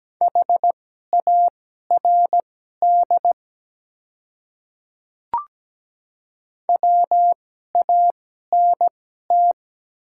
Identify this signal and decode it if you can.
Morse code, heard as a keyed tone, about 17 wpm, spaced as for 12 wpm, 700 Hz. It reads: HARD WANT
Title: 2x Character Spacing - Top 300 Words - 17wpm